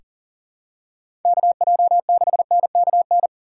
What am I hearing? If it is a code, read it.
Morse code, heard as a keyed tone, about 40 wpm, 700 Hz. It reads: KJ6NKN
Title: USA Call Signs - 6 Characters - 2x Character Spacing - 40wpm